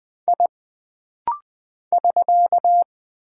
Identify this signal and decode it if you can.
Morse code, heard as a keyed tone, about 20 wpm, 700 Hz. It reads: I <SK>
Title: Single Character (Letter:Number:Punctuation:Prosign) - Rapid-Fire - 20wpm